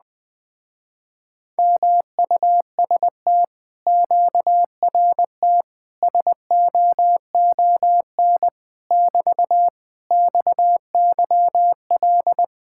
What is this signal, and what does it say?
Morse code, heard as a keyed tone, about 20 wpm, 700 Hz. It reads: MUST QRT SOON <BT> XYL
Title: Speed-Racing Common QSO Phrases Encore - 17wpm